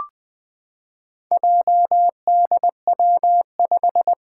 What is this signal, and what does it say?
Morse code, heard as a keyed tone, about 20 wpm, 700 Hz. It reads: JDW5
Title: Four Challenging Letter-Number Combinations 20wpm